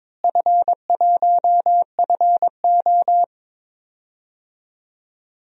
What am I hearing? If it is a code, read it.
Morse code, heard as a keyed tone, about 23 wpm, 700 Hz. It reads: F1FO
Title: DXCC-100 Call Signs - 4 Characters - Repeated 3x - 22wpm